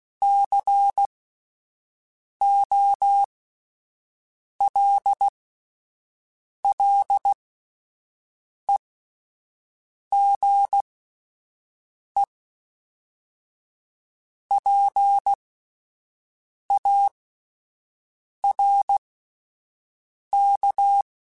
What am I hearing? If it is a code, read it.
Morse code, heard as a keyed tone, about 16 wpm, spaced as for 5 wpm, 800 Hz. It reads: COLLEGEPARK